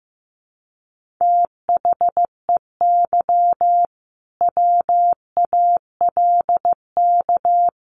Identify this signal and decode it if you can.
Morse code, heard as a keyed tone, about 15 wpm, 700 Hz. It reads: THEY WALK
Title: Sentences from Top 1500 Words 15wpm